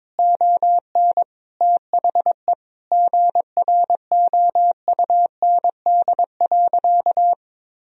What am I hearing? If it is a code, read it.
Morse code, heard as a keyed tone, about 22 wpm, 700 Hz. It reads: ON THE GROUND.
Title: Sentences from Top 1000 Words - Review Part 2 - 22wpm